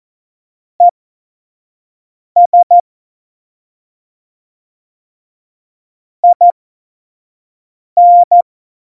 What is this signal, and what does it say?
Morse code, heard as a keyed tone, about 14 wpm, spaced as for 5 wpm, 700 Hz. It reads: ES IN
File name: Morsekurs / Witze / Witz 11 Juli 17, Tempo 25 F70